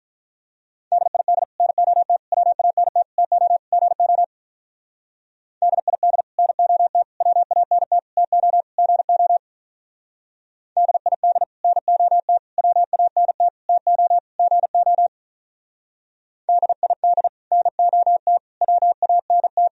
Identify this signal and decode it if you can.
Morse code, heard as a keyed tone, about 45 wpm, 700 Hz. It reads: DID NOT WANT TO GO DID NOT WANT TO GO DID NOT WANT TO GO DID NOT WANT